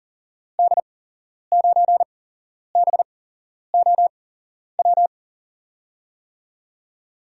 Morse code, heard as a keyed tone, about 42 wpm, spaced as for 11 wpm, 700 Hz. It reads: D9BOW
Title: USA Call Signs - 6 Characters - 8x Character Spacing - 40wpm